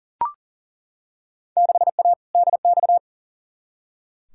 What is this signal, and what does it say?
Morse code, heard as a keyed tone, about 40 wpm, 700 Hz. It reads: 6A DX
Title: Field Day Exchanges Encore - 40wpm